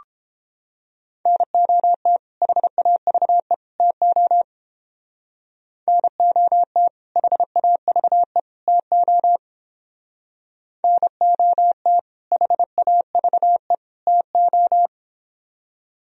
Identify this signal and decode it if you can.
Morse code, heard as a keyed tone, about 31 wpm, 700 Hz. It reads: NOT HAVE TO NOT HAVE TO NOT HAVE TO